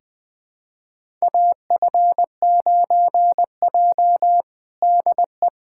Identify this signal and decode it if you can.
Morse code, heard as a keyed tone, about 20 wpm, 700 Hz. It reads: AF9J DE
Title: USA Callsign DE Callsign - 4 Characters - 20wpm